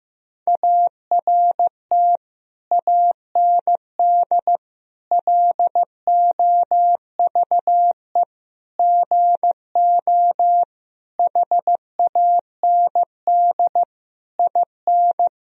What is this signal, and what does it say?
Morse code, heard as a keyed tone, about 16 wpm, 700 Hz. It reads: ART AND LOVE GO HAND IN
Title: Sentences from Top 1000 Words - Review Part 3 - 15wpm